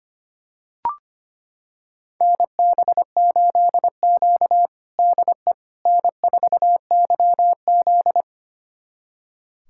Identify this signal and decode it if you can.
Morse code, heard as a keyed tone, about 25 wpm, 700 Hz. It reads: NB8Q DE N4YZ
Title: USA Callsign DE Callsign - 4 Characters - 25wpm